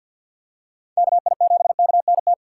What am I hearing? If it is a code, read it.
Morse code, heard as a keyed tone, about 47 wpm, 700 Hz. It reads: KI7KNT
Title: USA Call Signs - 6 Characters - 6x Character Spacing - 50wpm